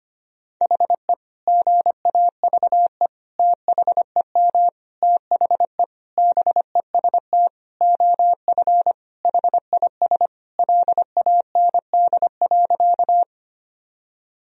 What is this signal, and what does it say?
Morse code, heard as a keyed tone, about 25 wpm, 700 Hz. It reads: HE GAVE THEM THE BEST OF HIS LAND.
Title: Sentences from Top 1000 Words - Review Part 1 - 25wpm